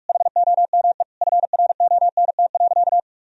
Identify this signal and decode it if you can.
Morse code, heard as a keyed tone, about 46 wpm, 700 Hz. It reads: HOME FRONT.
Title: Sentences from Top 1000 Words - Review Part 2 - 45wpm